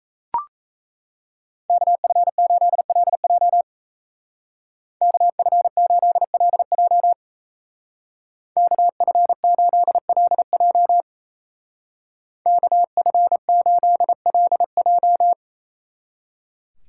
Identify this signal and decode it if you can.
Morse code, heard as a keyed tone, about 34 wpm, 700 Hz. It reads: KF8LJ KF8LJ KF8LJ KF8LJ
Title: Speed Racing - USA 2x2 Call Signs 28wpm